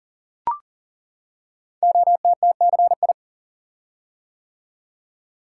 Morse code, heard as a keyed tone, about 39 wpm, 700 Hz. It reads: OTTCI
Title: Code Groups - 5 Letters - 40wpm